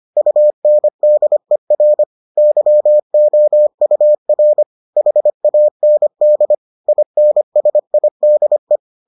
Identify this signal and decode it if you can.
Morse code, heard as a keyed tone, about 25 wpm, 600 Hz. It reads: UNDER YOUR HAND INSIDE